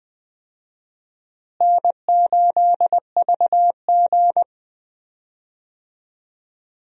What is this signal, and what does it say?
Morse code, heard as a keyed tone, about 20 wpm, 700 Hz. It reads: N8VG
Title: USA Call Signs - 4 Characters - Repeated 2x - 20wpm